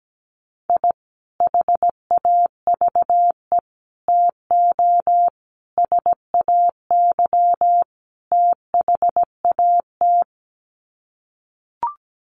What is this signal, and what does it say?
Morse code, heard as a keyed tone, about 17 wpm, 700 Hz. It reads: I HAVE TO SAY THAT
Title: Sets of 5 Words Encore 17wpm